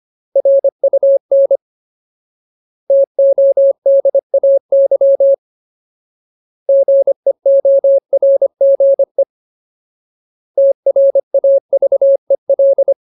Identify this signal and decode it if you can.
Morse code, heard as a keyed tone, about 25 wpm, 550 Hz. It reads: RUN TODAY GEORGE TRAVEL
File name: Ultimate Word List / 25WPM Ultimate